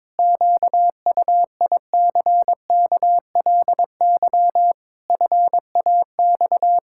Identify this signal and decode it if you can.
Morse code, heard as a keyed tone, about 22 wpm, 700 Hz. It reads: QUICKLY FAX
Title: Pangram Challenge v2 - 22wpm